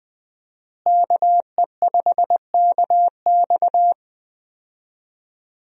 Morse code, heard as a keyed tone, about 20 wpm, 700 Hz. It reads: KE5KX
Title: USA Call Signs - 5 Characters - Repeated 3x - 20wpm